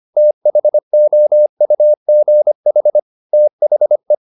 Morse code, heard as a keyed tone, about 25 wpm, 600 Hz. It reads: THOUGH THE
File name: the old man and the sea part 1 25wpm